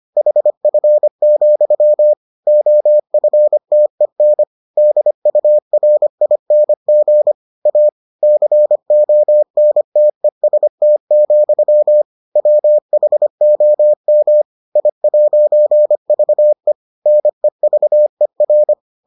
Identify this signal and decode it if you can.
Morse code, heard as a keyed tone, about 25 wpm, 600 Hz. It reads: HF, OFTEN DURING A CONTEST, WHOM I'VE NEVER